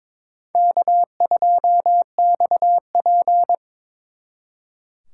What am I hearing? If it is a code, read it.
Morse code, heard as a keyed tone, about 26 wpm, 700 Hz. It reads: K2XP
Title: Speed Racing - USA Call Signs 22wpm